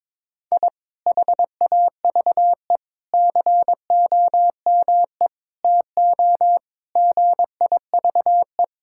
Morse code, heard as a keyed tone, about 22 wpm, 700 Hz. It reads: I HAVE COME TO GIVE
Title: Sentences from Top 1000 Words - Review Part 6 - 22wpm